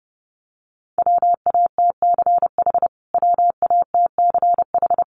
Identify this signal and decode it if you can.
Morse code, heard as a keyed tone, about 30 wpm, 700 Hz. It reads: WATCH WATCH
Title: Top 300 Words Repeated 30wpm